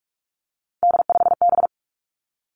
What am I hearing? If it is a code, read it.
Morse code, heard as a keyed tone, about 46 wpm, 700 Hz. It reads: D5B
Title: Three Challenging Letter-Number Combinations 45wpm